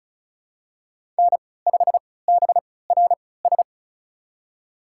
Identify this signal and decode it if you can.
Morse code, heard as a keyed tone, about 34 wpm, spaced as for 20 wpm, 700 Hz. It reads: N5BRS
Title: USA Call Signs - 5 Characters - 3x Character Spacing - 35wpm